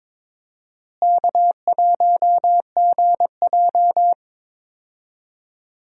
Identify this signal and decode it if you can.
Morse code, heard as a keyed tone, about 22 wpm, 700 Hz. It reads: K1GJ
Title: USA Call Signs - 4 Characters - Repeated 2x - 22wpm